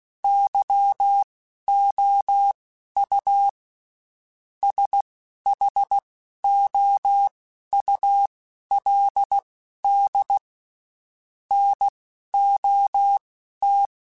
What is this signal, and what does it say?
Morse code, heard as a keyed tone, about 16 wpm, spaced as for 11 wpm, 800 Hz. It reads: YOU SHOULD NOT
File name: Spaced / Chuckles Spaced-mp3s / Temptation sp @15